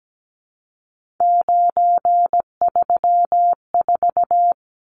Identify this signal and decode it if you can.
Morse code, heard as a keyed tone, about 17 wpm, 700 Hz. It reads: 934
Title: Numbers - 3-Digit 17wpm